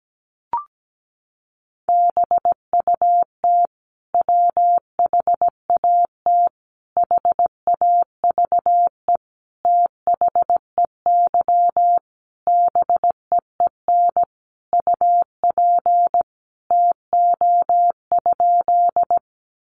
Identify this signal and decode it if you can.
Morse code, heard as a keyed tone, about 17 wpm, 700 Hz. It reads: BUT WHAT HAVE THEY BEEN UP TO?
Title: Sentences from Top 100 Words 17wpm